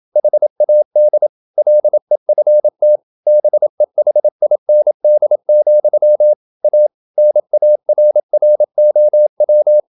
Morse code, heard as a keyed tone, about 27 wpm, 600 Hz. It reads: HAD LEFT BEHIND, A NARROW
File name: Never 0027wpm